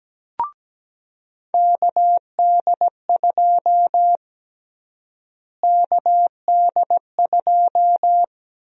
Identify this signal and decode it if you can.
Morse code, heard as a keyed tone, about 17 wpm, 700 Hz. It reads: KD2 KD2
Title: USA Call Sign Prefixes - Repeated 2x - 17wpm